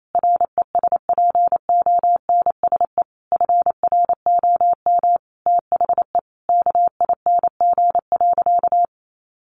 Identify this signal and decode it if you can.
Morse code, heard as a keyed tone, about 28 wpm, 700 Hz. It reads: RESPONSE FROM THE KING.
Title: Sentences from Top 1400 Words 28wpm